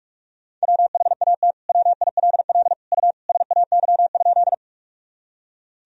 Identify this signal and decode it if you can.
Morse code, heard as a keyed tone, about 41 wpm, 700 Hz. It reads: WHAT WILL U SAY?
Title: Sentences from Top 200 Words and Abbreviations 45wpm